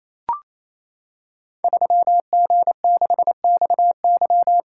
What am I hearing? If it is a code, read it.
Morse code, heard as a keyed tone, about 27 wpm, 700 Hz. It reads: 3G6XY
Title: Code Groups - 5 Letter:Numbers - 28wpm